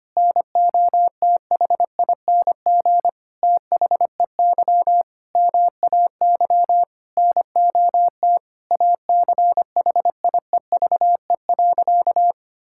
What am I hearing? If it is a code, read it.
Morse code, heard as a keyed tone, about 25 wpm, 700 Hz. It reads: NOTHING THEY MAY NOT ACHIEVE.
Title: Sentences from Top 1000 Words - Review Part 6 - 25wpm